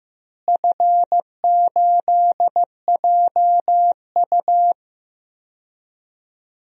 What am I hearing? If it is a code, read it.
Morse code, heard as a keyed tone, about 15 wpm, 700 Hz. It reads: F8JU
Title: DXCC-100 Call Signs - 4 Characters - Repeated 2x - 15wpm